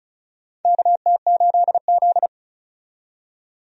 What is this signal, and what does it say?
Morse code, heard as a keyed tone, about 36 wpm, 700 Hz. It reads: KT8Z
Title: USA Call Signs - 4 Characters - 35wpm